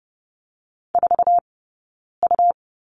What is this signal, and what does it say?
Morse code, heard as a keyed tone, about 30 wpm, spaced as for 9 wpm, 690 Hz. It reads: 4U
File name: Lesson 19d - Letter w - Two Characters 30wpm